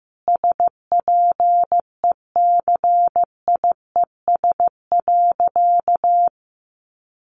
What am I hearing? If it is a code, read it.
Morse code, heard as a keyed tone, about 15 wpm, 700 Hz. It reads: SPECIES.
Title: A Sound Of Thunder 15wpm